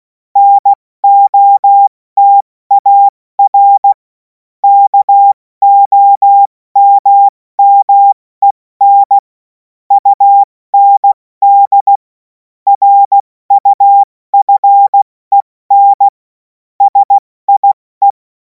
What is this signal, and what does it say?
Morse code, heard as a keyed tone, about 16 wpm, spaced as for 14 wpm, 800 Hz. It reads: NOTAR KOMMEN UND RUFEN SIE
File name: Morsekurs / Witze / Witz 07. Juli 2026 70 F80